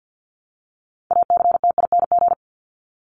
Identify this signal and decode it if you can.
Morse code, heard as a keyed tone, about 50 wpm, 700 Hz. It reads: ACTING
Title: ING Suffix - Top 100 Words with Base Words 50wpm